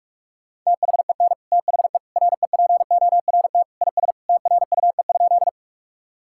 Morse code, heard as a keyed tone, about 41 wpm, 700 Hz. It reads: THEN THE REPORT IS TRUE?
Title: Sentences from Top 1000 Words - Review Part 6 - 45wpm